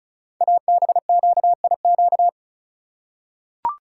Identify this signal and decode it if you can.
Morse code, heard as a keyed tone, about 36 wpm, 690 Hz. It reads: ABQIQ
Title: Code Groups - 5 Letters - 35wpm